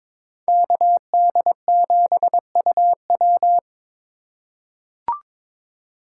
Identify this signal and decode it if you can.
Morse code, heard as a keyed tone, about 22 wpm, 700 Hz. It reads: KD7UW
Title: USA Call Signs - 5 Characters - 4x Character Spacing - 22wpm